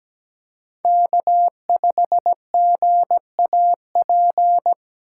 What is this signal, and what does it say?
Morse code, heard as a keyed tone, about 17 wpm, 700 Hz. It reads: K5GAP
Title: USA Call Signs - 5 Characters - 3x Character Spacing - 17wpm